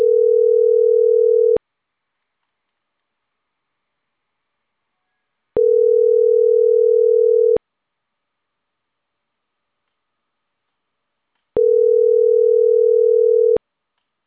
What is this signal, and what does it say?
Ringback tone (North America), 440 + 480 Hz.